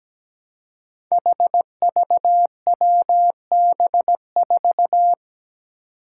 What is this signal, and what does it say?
Morse code, heard as a keyed tone, about 17 wpm, 700 Hz. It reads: HVWB4
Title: Five Challenging Letter-Number Combinations 17wpm